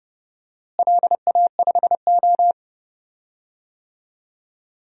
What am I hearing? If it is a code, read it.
Morse code, heard as a keyed tone, about 31 wpm, 700 Hz. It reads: LA5O
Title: DXCC-100 Call Signs - 4 Characters - Repeated 3x - 30wpm